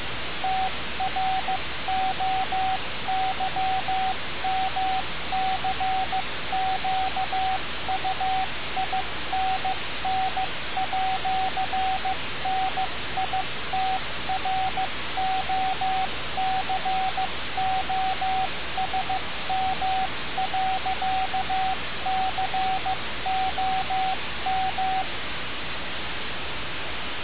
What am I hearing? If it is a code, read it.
Morse code, heard as a keyed tone, about 15 wpm, 760 Hz, with radio noise. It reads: TROYMCQUINN@NITROCOSM.COM
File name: radiocap / REC z